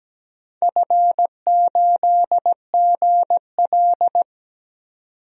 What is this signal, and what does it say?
Morse code, heard as a keyed tone, about 17 wpm, 700 Hz. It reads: F8GL
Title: DXCC-100 Call Signs - 4 Characters - 17wpm